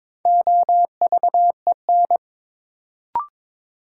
Morse code, heard as a keyed tone, about 22 wpm, 700 Hz. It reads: OVEN
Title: Sentences from Top 1000 Words - Review Part 4 - New Word Drill - 22wpm